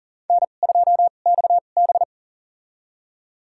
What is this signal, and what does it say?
Morse code, heard as a keyed tone, about 40 wpm, spaced as for 29 wpm, 700 Hz. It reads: N2XB
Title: USA Call Signs - 4 Characters - 2x Character Spacing - 40wpm